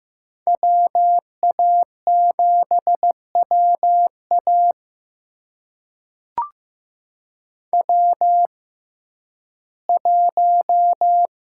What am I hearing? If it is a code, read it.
Morse code, heard as a keyed tone, about 15 wpm, spaced as for 11 wpm, 700 Hz. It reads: WA7WA W1
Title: USA Call Signs - 5 Characters - 6x Character Spacing - 15wpm